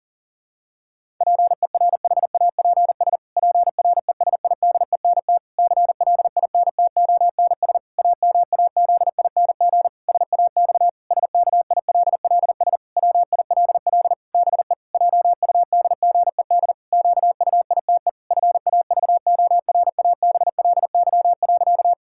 Morse code, heard as a keyed tone, about 40 wpm, 700 Hz. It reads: PERHAPS PRESIDENT CLINTONS AMAZING SAX SKILLS WILL BE JUDGED QUITE FAVORABLY.
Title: Pangram Challenge v2 - 40wpm